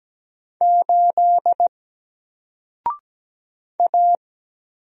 Morse code, heard as a keyed tone, about 17 wpm, 700 Hz. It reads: 8 A
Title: Single Character (Letter:Number:Punctuation:Prosign) Alt 17wpm